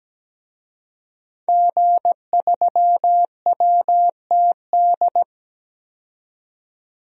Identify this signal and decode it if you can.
Morse code, heard as a keyed tone, about 17 wpm, 700 Hz. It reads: G3WTD
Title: DXCC-100 Call Signs - 5 Characters - 17wpm